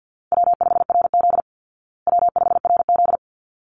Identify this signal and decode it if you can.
Morse code, heard as a keyed tone, about 50 wpm, 700 Hz. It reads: W5RZ W5RZ
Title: Easy Call Signs Repeated 2x - 50wpm